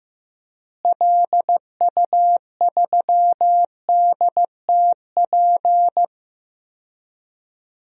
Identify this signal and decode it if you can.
Morse code, heard as a keyed tone, about 16 wpm, 700 Hz. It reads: LU3DTP
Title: DXCC-100 Call Signs - 6 Characters - Repeated 3x - 15wpm